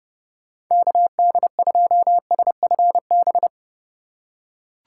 Morse code, heard as a keyed tone, about 32 wpm, 700 Hz. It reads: KD2SFB
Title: USA Call Signs - 6 Characters - 2x Character Spacing - 30wpm